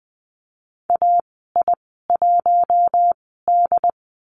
Morse code, heard as a keyed tone, about 20 wpm, spaced as for 15 wpm, 700 Hz. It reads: AI1D
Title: 082 Lesson 14g - Number 2 - Call Signs 20wpm